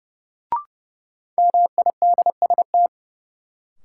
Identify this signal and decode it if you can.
Morse code, heard as a keyed tone, about 31 wpm, 700 Hz. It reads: MIDST
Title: Sets of 4 Words - New Word Drill - 30wpm